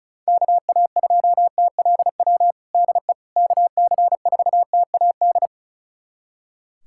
Morse code, heard as a keyed tone, about 36 wpm, 700 Hz. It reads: KA2TLW DE KC4TAD
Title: USA Callsign DE Callsign - 6 Characters - 35wpm